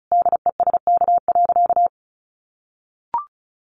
Morse code, heard as a keyed tone, about 34 wpm, 700 Hz. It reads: DESK.
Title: A Sound Of Thunder 35wpm